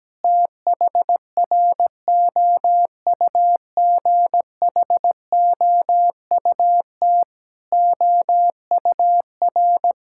Morse code, heard as a keyed tone, about 17 wpm, 700 Hz. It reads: THROUGHOUT OUR
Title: Sentences from Top 1000 Words - Review Part 5 - 17wpm